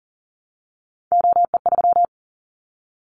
Morse code, heard as a keyed tone, about 40 wpm, 700 Hz. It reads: OE3
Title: Callsign Prefixes for Top 100 DXCC - 40wpm